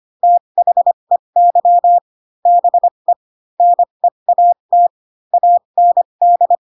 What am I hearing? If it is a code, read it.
Morse code, heard as a keyed tone, about 25 wpm, spaced as for 22 wpm, 700 Hz. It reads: THEY BE NEAT AND